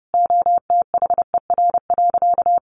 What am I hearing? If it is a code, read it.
Morse code, heard as a keyed tone, about 30 wpm, 700 Hz. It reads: OTHER.
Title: Easy Phrases 30wpm